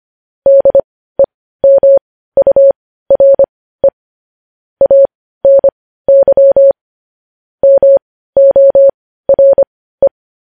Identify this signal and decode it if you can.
Morse code, heard as a keyed tone, about 25 wpm, spaced as for 15 wpm, 550 Hz. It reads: DEMURE ANY MORE
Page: News Headlines in Morse Code at 15 WPM